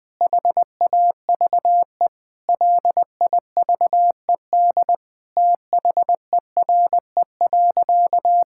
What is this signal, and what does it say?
Morse code, heard as a keyed tone, about 20 wpm, 700 Hz. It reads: HAVE LIVED THERE.
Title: Sentences with 7-Word Phrases - 20wpm